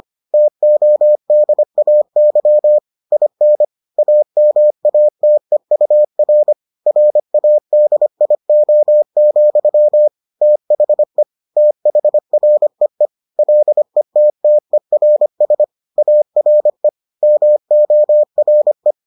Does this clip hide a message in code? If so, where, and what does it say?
Yes: Morse code from 0.3 to 18.9 s, heard as a keyed tone, about 25 wpm, 600 Hz. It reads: TODAY IN AMATEUR RADIO, THE THREE LETTERS ARE MORE